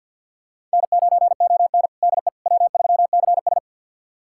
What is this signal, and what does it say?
Morse code, heard as a keyed tone, about 51 wpm, 700 Hz. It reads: N9ON DE W3CS
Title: USA Callsign DE Callsign - 4 Characters - 50wpm